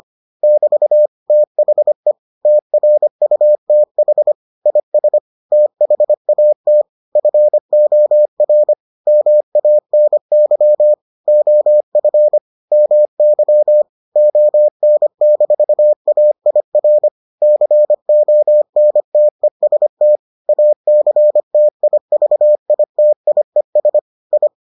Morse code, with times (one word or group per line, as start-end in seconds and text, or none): 0.4-1.1 <BT>
1.3-2.1 THE
2.5-4.3 TRUTH
4.7-5.2 IS
5.5-6.8 THAT
7.2-8.7 FOR
9.1-10.9 MANY
11.3-12.4 OF
12.7-13.8 MY
14.2-17.1 ON-AIR
17.4-20.2 CONTEST
20.5-24.0 ACTIVITIES
24.3-24.5 I